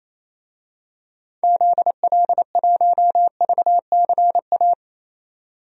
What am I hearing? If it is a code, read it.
Morse code, heard as a keyed tone, about 28 wpm, 700 Hz. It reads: ZL1VCA
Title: DXCC-100 Call Signs - 6 Characters - 28wpm